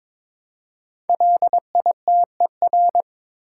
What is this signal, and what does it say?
Morse code, heard as a keyed tone, about 22 wpm, 700 Hz. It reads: LITER
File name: 5 Letter Words - First 3 Letters Given - 22wpm